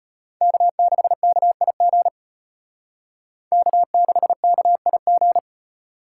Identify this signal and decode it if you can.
Morse code, heard as a keyed tone, about 31 wpm, 700 Hz. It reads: K6KIG K6KIG
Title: Speed Racing - USA 1x3 Call Signs 25wpm